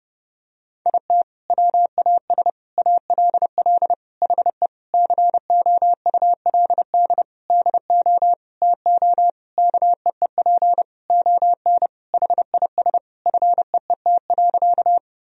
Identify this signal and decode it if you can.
Morse code, heard as a keyed tone, about 30 wpm, 700 Hz. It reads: IT WAS ALL HE COULD DO TO KEEP ON HIS FEET.
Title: Sentences from Top 1000 Words - Review Part 1 - 30wpm